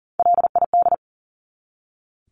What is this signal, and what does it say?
Morse code, heard as a keyed tone, about 40 wpm, 700 Hz. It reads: LID
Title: Abbreviations 40wpm